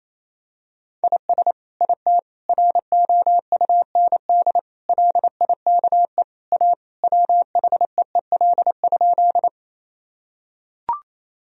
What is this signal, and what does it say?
Morse code, heard as a keyed tone, about 28 wpm, 700 Hz. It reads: IS IT ROUND LIKE A WHEEL?
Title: Sentences from Top 1000 Words - Review Part 3 - 28wpm